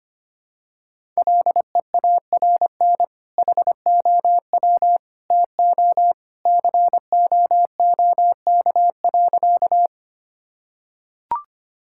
Morse code, heard as a keyed tone, about 25 wpm, 700 Hz. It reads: LEARN HOW TO COOK.
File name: practice / 25wpm / Sentences with 2-Word Phrases - 25wpm